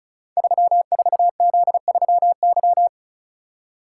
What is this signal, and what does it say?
Morse code, heard as a keyed tone, about 36 wpm, 700 Hz. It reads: 34Z3Y
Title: Code Groups - 5 Letter:Numbers - 35wpm